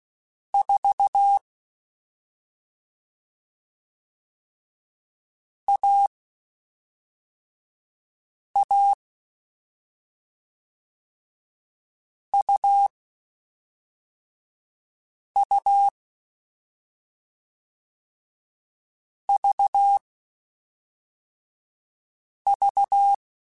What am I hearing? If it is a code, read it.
Morse code, heard as a keyed tone, about 16 wpm, 800 Hz. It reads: 4 A A U U V V